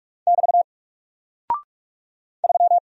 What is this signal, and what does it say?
Morse code, heard as a keyed tone, about 44 wpm, 700 Hz. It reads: <BT> 3
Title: Single Character (Letter:Number:Punctuation:Prosign) 45wpm